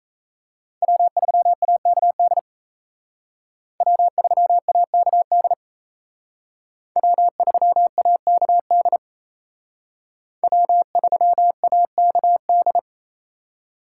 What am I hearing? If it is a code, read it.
Morse code, heard as a keyed tone, about 36 wpm, 700 Hz. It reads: W3AKD W3AKD W3AKD W3AKD